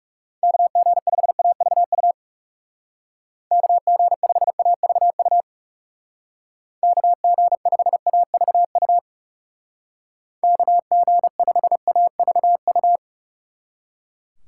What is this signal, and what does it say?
Morse code, heard as a keyed tone, about 36 wpm, 700 Hz. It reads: KG5AVU KG5AVU KG5AVU KG5AVU
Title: Speed Racing - USA 2x3 Call Signs 30wpm